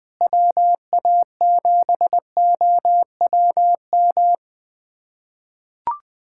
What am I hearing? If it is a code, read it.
Morse code, heard as a keyed tone, about 21 wpm, 700 Hz. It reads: WA7OWM
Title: USA Callsign DE Callsign 20wpm